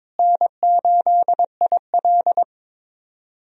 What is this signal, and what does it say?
Morse code, heard as a keyed tone, about 22 wpm, 700 Hz. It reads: N8IL
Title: USA Call Signs - 4 Characters - Repeated 2x - 22wpm